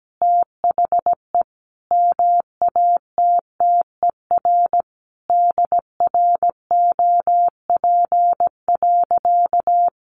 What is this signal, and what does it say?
Morse code, heard as a keyed tone, about 17 wpm, 700 Hz. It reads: THE MATTER DROP.
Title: Sentences from Top 1400 Words 17wpm